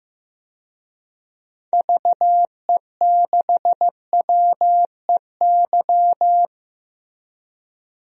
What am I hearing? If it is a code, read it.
Morse code, heard as a keyed tone, about 15 wpm, 700 Hz. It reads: VE6WEY
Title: DXCC-100 Call Signs - 6 Characters - Repeated 2x - 15wpm